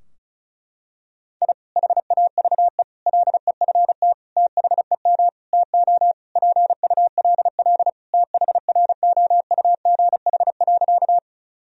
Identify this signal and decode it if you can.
Morse code, heard as a keyed tone, about 36 wpm, 700 Hz. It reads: I HAVE LEFT THEM TO PULL THROUGH.
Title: Sentences from Top 1000 Words - Review Part 2 - 35wpm